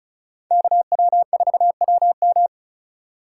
Morse code, heard as a keyed tone, about 39 wpm, 700 Hz. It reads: KW4WM
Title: Speed Racing - USA Call Signs 35wpm